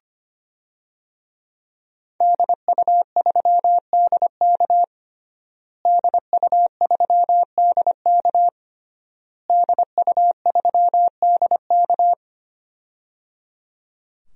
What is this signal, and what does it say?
Morse code, heard as a keyed tone, about 25 wpm, 700 Hz. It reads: DU3DK DU3DK DU3DK
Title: DXCC-100 Call Signs - 5 Characters - Repeated 3x - 25wpm